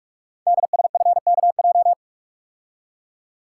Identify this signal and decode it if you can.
Morse code, heard as a keyed tone, about 44 wpm, 700 Hz. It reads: DSFKJ
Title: Code Groups - 5 Letters - 45wpm